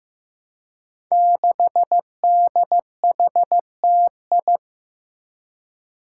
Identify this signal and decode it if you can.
Morse code, heard as a keyed tone, about 15 wpm, 700 Hz. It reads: 6DHTI
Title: Five Challenging Letter-Number Combinations 15wpm